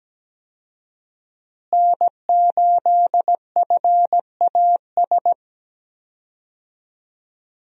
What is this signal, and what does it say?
Morse code, heard as a keyed tone, about 17 wpm, 700 Hz. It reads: N8FAS
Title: USA Call Signs - 5 Characters - Repeated 2x - 17wpm